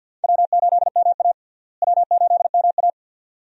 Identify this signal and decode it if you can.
Morse code, heard as a keyed tone, about 52 wpm, 700 Hz. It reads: W8MA W8MA
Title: USA Call Signs - 4 Characters - Repeated 2x - 50wpm